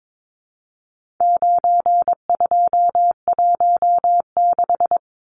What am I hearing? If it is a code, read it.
Morse code, heard as a keyed tone, about 22 wpm, 700 Hz. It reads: 9216
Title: Numbers - 4-Digit 22wpm